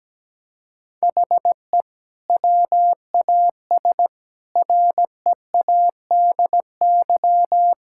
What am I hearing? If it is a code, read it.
Morse code, heard as a keyed tone, about 17 wpm, 700 Hz. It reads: HE WAS READY
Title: Sentences from Top 1000 Words - Review Part 5 - 17wpm